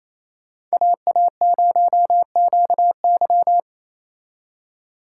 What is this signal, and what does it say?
Morse code, heard as a keyed tone, about 35 wpm, 700 Hz. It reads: AA0QY
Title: Speed Racing - USA 2x2 Call Signs 28wpm